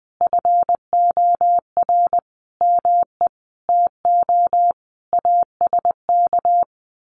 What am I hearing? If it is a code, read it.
Morse code, heard as a keyed tone, about 20 wpm, 700 Hz. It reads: FOR ME TO ASK
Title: Sentences from Top 1500 Words 20wpm